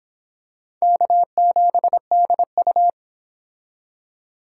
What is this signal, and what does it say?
Morse code, heard as a keyed tone, about 25 wpm, 700 Hz. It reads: K7DU